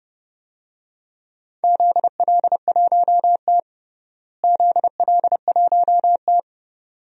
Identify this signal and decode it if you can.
Morse code, heard as a keyed tone, about 30 wpm, 700 Hz. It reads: ZL1T ZL1T
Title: DXCC-100 Call Signs - 4 Characters - Repeated 2x - 30wpm